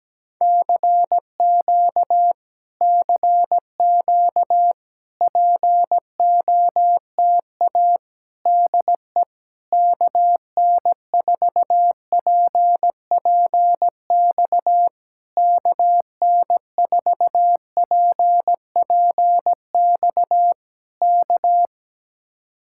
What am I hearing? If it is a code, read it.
Morse code, heard as a keyed tone, about 17 wpm, 700 Hz. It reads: CQ CQ POTA DE KN4PPX KN4PPX K
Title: Speed-Racing Common QSO Phrases - 17wpm